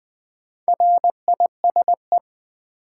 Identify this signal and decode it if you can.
Morse code, heard as a keyed tone, about 20 wpm, 700 Hz. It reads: RISE
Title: Sentences from Top 900 Words - New Word Drill - 20wpm